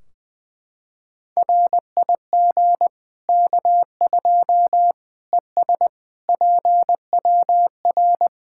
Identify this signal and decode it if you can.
Morse code, heard as a keyed tone, about 20 wpm, 700 Hz. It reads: RIG K2 ES PWR
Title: Practice QSO - Exchange 2 - Info - 20wpm